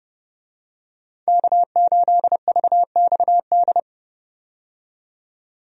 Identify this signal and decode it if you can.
Morse code, heard as a keyed tone, about 30 wpm, 700 Hz. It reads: K8VXD
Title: USA Call Signs - 5 Characters - 30wpm